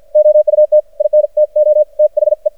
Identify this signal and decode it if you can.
Morse code, heard as a keyed tone, about 53 wpm, spaced as for 46 wpm, 610 Hz. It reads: OUT INTO THE
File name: archived / CRSnet